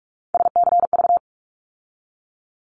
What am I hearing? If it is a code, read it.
Morse code, heard as a keyed tone, about 44 wpm, 700 Hz. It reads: SCV
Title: ARRL Sections 45wpm